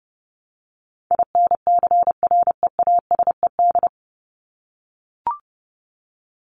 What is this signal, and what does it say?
Morse code, heard as a keyed tone, about 30 wpm, 700 Hz. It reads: INCREASED
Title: ED Suffix - Top 100 Words with Base Words 30wpm